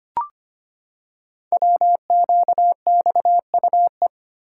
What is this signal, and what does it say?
Morse code, heard as a keyed tone, about 25 wpm, 700 Hz. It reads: WQXUE
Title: Code Groups - 5 Letters - 25wpm